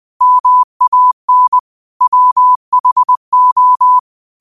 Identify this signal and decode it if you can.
Morse code, heard as a keyed tone, about 20 wpm, 1000 Hz. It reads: MAN WHO
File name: qotdfiles / 20wpm